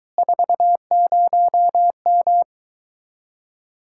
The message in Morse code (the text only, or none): 40M